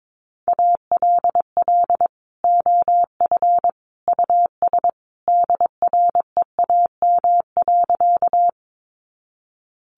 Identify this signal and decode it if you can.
Morse code, heard as a keyed tone, about 22 wpm, 700 Hz. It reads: ALL OF US DREAM.
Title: Sentences from Top 1500 Words 22wpm